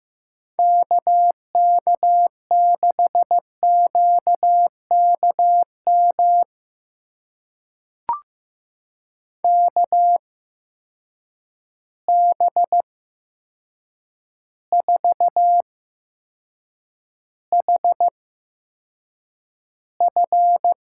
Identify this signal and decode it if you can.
Morse code, heard as a keyed tone, about 15 wpm, spaced as for 7 wpm, 700 Hz. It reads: KK6QKM KB4HF
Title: USA Call Signs - 6 Characters - 8x Character Spacing - 15wpm